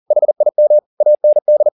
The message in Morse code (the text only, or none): HIM AND